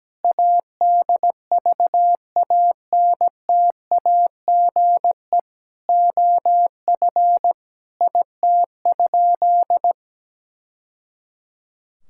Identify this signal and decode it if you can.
Morse code, heard as a keyed tone, about 17 wpm, 700 Hz. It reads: ADVANTAGE OF IT?
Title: Sentences from Top 1000 Words - Review Part 6 - 17wpm